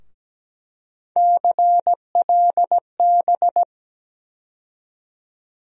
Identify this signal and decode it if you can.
Morse code, heard as a keyed tone, about 17 wpm, 700 Hz. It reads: CLB
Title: USA Call Sign Postfixes - 3 Letters 17wpm